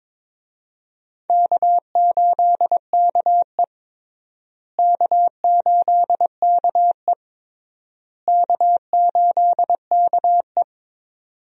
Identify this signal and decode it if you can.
Morse code, heard as a keyed tone, about 22 wpm, 700 Hz. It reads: K8KE K8KE K8KE
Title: USA Call Signs - 4 Characters - Repeated 3x - 22wpm